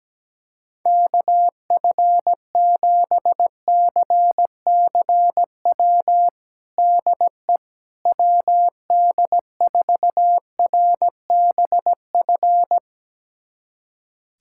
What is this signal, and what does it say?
Morse code, heard as a keyed tone, about 17 wpm, 700 Hz. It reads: KF7CCW DE WD4RBF
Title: USA Callsign DE Callsign - 6 Characters - 17wpm